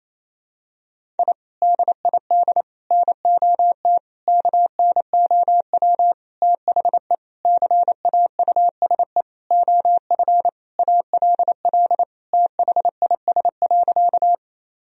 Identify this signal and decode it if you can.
Morse code, heard as a keyed tone, about 28 wpm, 700 Hz. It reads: I DID NOT KNOW THE CAUSE OF ALL THIS.
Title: Sentences from Top 1000 Words - Review Part 1 - 28wpm